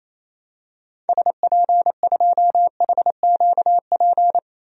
Morse code, heard as a keyed tone, about 28 wpm, 700 Hz. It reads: SP2HQP
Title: DXCC-100 Call Signs - 6 Characters - Repeated 3x - 28wpm